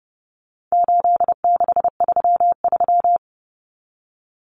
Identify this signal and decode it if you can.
Morse code, heard as a keyed tone, about 28 wpm, 700 Hz. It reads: 8633
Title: Numbers - 4-Digit 30wpm